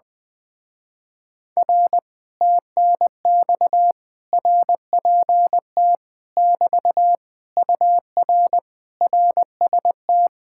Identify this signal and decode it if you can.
Morse code, heard as a keyed tone, about 21 wpm, 700 Hz. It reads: R TNX RPT <BT> UR RST
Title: Speed-Racing Common QSO Phrases Encore - 20wpm